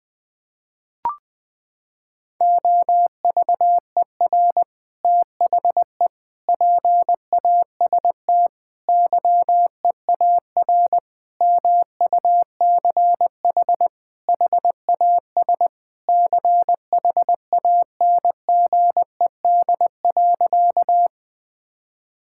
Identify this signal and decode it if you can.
Morse code, heard as a keyed tone, about 20 wpm, 700 Hz. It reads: OVER THE PAST YEAR MUCH HAS CHANGED.
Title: Sentences with 3-Word Phrases - 20wpm